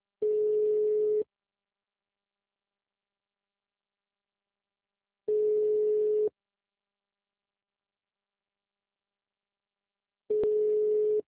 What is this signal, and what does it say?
Ringback tone (Europe, 425 Hz).
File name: out- -2031- - - .54022